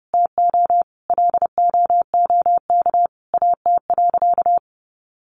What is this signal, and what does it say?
Morse code, heard as a keyed tone, about 30 wpm, 700 Hz. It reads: TO LOOK AT.
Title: Sentences from Top 200 Words 30wpm